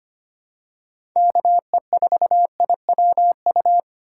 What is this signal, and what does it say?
Morse code, heard as a keyed tone, about 25 wpm, 700 Hz. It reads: KE4IWU
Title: USA Call Signs - 6 Characters - 4x Character Spacing - 25wpm